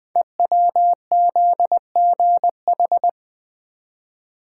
Morse code, heard as a keyed tone, about 20 wpm, 700 Hz. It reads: EWZGH
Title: Code Groups - 5 Letters - 20wpm